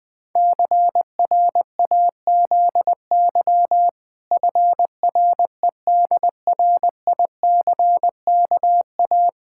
Morse code, heard as a keyed tone, about 20 wpm, 700 Hz. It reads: CRAZY FREDRICKA
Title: Pangram Challenge v2 - 20wpm